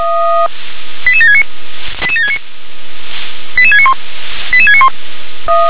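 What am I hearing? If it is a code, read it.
Five-tone selective call (ZVEI), heard as quick runs of five tones, tones of about 70 ms. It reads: 88769 88769 88691 88691
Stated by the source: Received by SDR.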